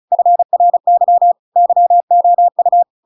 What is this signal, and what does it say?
Morse code, heard as a keyed tone, about 34 wpm, 700 Hz. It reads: FRY YOU